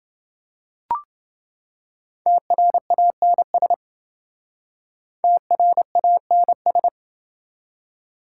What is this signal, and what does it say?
Morse code, heard as a keyed tone, about 28 wpm, 700 Hz. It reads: TRANS TRANS